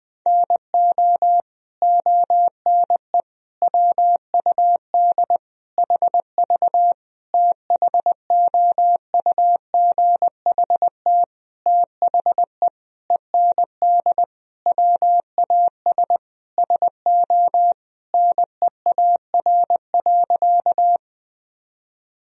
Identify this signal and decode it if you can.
Morse code, heard as a keyed tone, about 20 wpm, 700 Hz. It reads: NO ONE WUD HV THOUGHT THE END WAS SO NEAR.
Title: Sentences from Top 300 Words and Abbreviations 20wpm